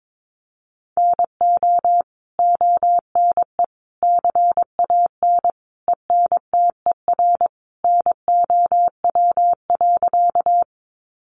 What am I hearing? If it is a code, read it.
Morse code, heard as a keyed tone, about 22 wpm, 700 Hz. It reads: NO ONE CAN ENTER NOW.
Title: Sentences from Top 1300 Words 22wpm